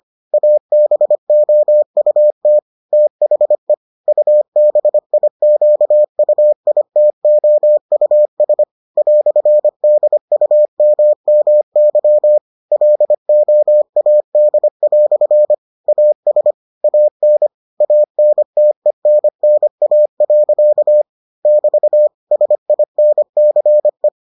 Morse code, heard as a keyed tone, about 25 wpm, 600 Hz. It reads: ABOUT THE UBIQUITOUS "DUMMY LOAD" AS AN ANTENNA. <BT> SINCE